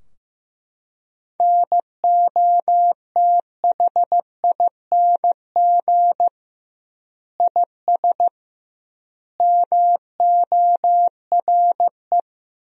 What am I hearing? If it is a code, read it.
Morse code, heard as a keyed tone, about 15 wpm, 700 Hz. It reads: NOTHING IS MORE
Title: 2x Word-Spacing - Sentences from Top 500 Words - 15wpm